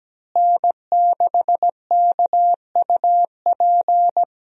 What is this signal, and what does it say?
Morse code, heard as a keyed tone, about 17 wpm, 700 Hz. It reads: N6KUP
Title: USA Call Signs - 5 Characters - Repeated 3x - 17wpm